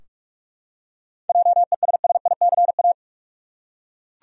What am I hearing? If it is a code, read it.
Morse code, heard as a keyed tone, about 42 wpm, 700 Hz. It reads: JESSICA
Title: USA Common Names - 45wpm